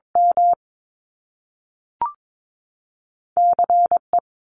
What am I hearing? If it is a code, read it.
Morse code, heard as a keyed tone, about 22 wpm, 700 Hz. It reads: M C E
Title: Top Two Letter Combinations in Words 22wpm